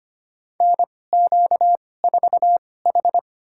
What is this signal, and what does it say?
Morse code, heard as a keyed tone, about 25 wpm, spaced as for 18 wpm, 700 Hz. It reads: NQ4H
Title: USA Call Signs - 4 Characters - 2x Character Spacing - 25wpm